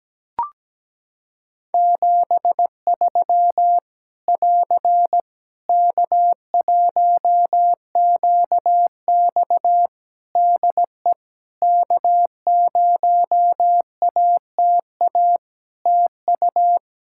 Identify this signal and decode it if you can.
Morse code, heard as a keyed tone, about 17 wpm, 700 Hz. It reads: 73 <AR> K1QX DE K0ATA TU
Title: Practice QSO - Exchange 4 - Ending - 17wpm